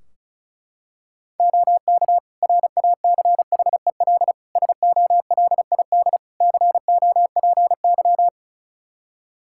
Morse code, heard as a keyed tone, about 36 wpm, 700 Hz. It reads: OK RACHEL SOLID COPY
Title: K1USN SST QSO Practice 35wpm